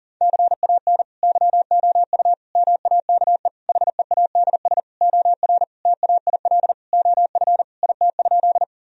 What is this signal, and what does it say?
Morse code, heard as a keyed tone, about 40 wpm, 700 Hz. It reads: CAN YOU MAKE HEADS OR TAIL OF IT?
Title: Sentences from Top 1000 Words - Review Part 3 - 40wpm